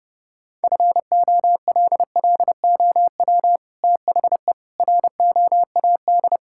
Morse code, heard as a keyed tone, about 30 wpm, 700 Hz. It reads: FOLLOW THE ROAD
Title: Sentences from Top 1000 Words - Review Part 6 - 30wpm